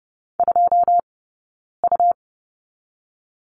Morse code, heard as a keyed tone, about 30 wpm, 700 Hz. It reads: 2 U
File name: Lesson 17d - Letter u - Two Characters 30wpm